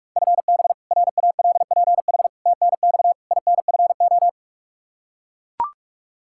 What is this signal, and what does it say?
Morse code, heard as a keyed tone, about 42 wpm, 700 Hz. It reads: FB RALPH TNX INFO
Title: Practice QSO - Exchange 3 - More Info - 45wpm